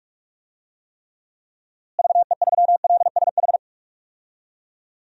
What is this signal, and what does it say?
Morse code, heard as a keyed tone, about 41 wpm, 700 Hz. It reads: VE3LSH
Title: DXCC-100 Call Signs - 6 Characters - Repeated 3x - 45wpm